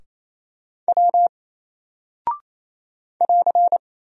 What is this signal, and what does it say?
Morse code, heard as a keyed tone, about 27 wpm, 700 Hz. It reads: W <AR>
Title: Single Character (Letter:Number:Punctuation:Prosign) 28wpm